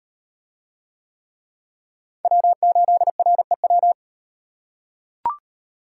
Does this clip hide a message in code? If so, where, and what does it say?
Yes: Morse code from 2.2 to 3.9 s, heard as a keyed tone, about 38 wpm, 700 Hz. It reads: W8REW